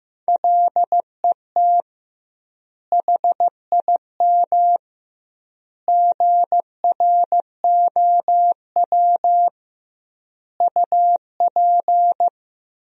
Morse code, heard as a keyed tone, about 16 wpm, 700 Hz. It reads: LET HIM GROW UP
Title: 2x Word-Spacing - Sentences from Top 300 Words - 15wpm